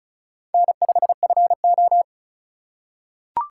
Morse code, heard as a keyed tone, about 36 wpm, 700 Hz. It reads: N5FO
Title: USA Call Signs - 4 Characters - 3x Character Spacing - 35wpm